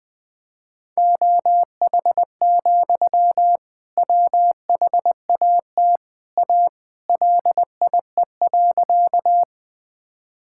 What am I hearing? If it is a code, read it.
Morse code, heard as a keyed tone, about 19 wpm, 700 Hz. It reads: OH, WHAT A LIE.
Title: Sentences from Top 1600 Words 20wpm